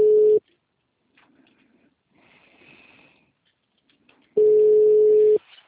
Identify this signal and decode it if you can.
Ringback tone (Europe, 425 Hz).
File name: out- -2002- - - .97347